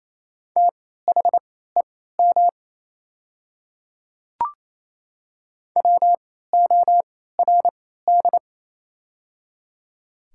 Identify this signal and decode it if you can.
Morse code, heard as a keyed tone, about 27 wpm, spaced as for 16 wpm, 690 Hz. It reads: THEM WORD
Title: 3x Character Spacing - Top 100 Words - 28wpm